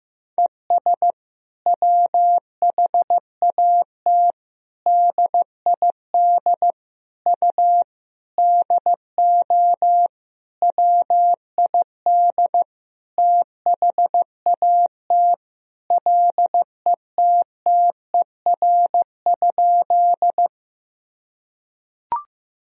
Morse code, heard as a keyed tone, about 15 wpm, 700 Hz. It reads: ES WHAT DID U DO WID THAT LETTER?
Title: Sentences from Top 300 Words and Abbreviations 15wpm